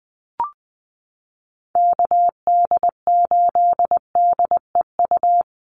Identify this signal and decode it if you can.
Morse code, heard as a keyed tone, about 20 wpm, 690 Hz. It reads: KD8DEU
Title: NA SOTA Chaser Callsigns 20wpm